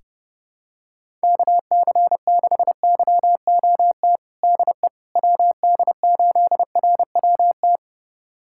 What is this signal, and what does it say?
Morse code, heard as a keyed tone, about 30 wpm, 700 Hz. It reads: KC6YOT DE WD8RWT
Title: USA Callsign DE Callsign 30wpm